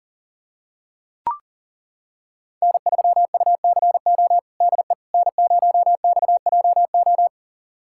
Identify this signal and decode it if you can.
Morse code, heard as a keyed tone, about 40 wpm, 700 Hz. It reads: N3UCO DE N0XJO
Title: USA Callsign DE Callsign - 5 Characters - 40wpm